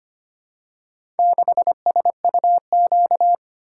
Morse code, heard as a keyed tone, about 25 wpm, 700 Hz. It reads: 6SUQ
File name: practice / 25wpm / Four Letter-Number Combinations - 25wpm